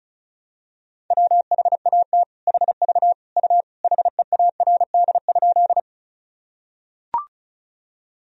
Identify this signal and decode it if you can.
Morse code, heard as a keyed tone, about 36 wpm, 700 Hz. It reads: WHAT HV U HEARD?
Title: Sentences from Top 400 Words and Abbreviations 35wpm